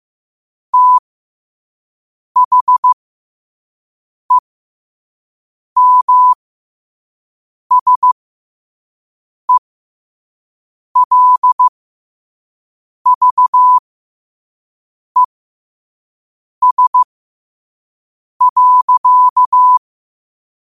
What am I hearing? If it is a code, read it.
Morse code, heard as a keyed tone, about 15 wpm, spaced as for 5 wpm, 1000 Hz. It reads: THEMSELVES.
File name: qotdfiles / 5wpm